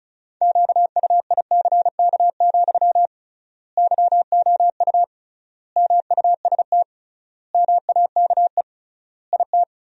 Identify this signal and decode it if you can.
Morse code, heard as a keyed tone, about 36 wpm, 700 Hz. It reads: QUICK, YOU MUST MAKE IT
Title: 3x Word-Spacing - Sentences from Top 500 Words - 35wpm